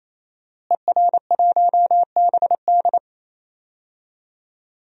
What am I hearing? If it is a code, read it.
Morse code, heard as a keyed tone, about 28 wpm, 700 Hz. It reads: ER1BD
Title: DXCC-100 Call Signs - 5 Characters - Repeated 3x - 28wpm